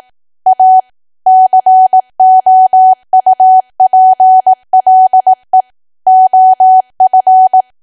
Morse code, heard as a keyed tone, about 18 wpm, 750 Hz. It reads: A COUPLE OF
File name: Archive / 18 WPM / 18WPM